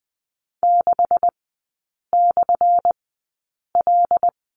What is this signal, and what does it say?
Morse code, heard as a keyed tone, about 20 wpm, spaced as for 8 wpm, 700 Hz. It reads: 6/L
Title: 178 Lesson 29e - Slash - Three Characters 20wpm